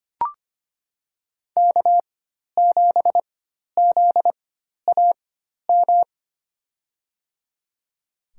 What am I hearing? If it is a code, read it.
Morse code, heard as a keyed tone, about 25 wpm, spaced as for 12 wpm, 700 Hz. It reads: K7ZAM
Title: USA Call Signs - 5 Characters - 4x Character Spacing - 25wpm